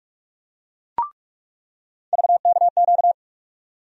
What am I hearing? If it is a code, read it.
Morse code, heard as a keyed tone, about 46 wpm, 700 Hz. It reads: VKQ
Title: USA Call Sign Postfixes - 3 Letters 45wpm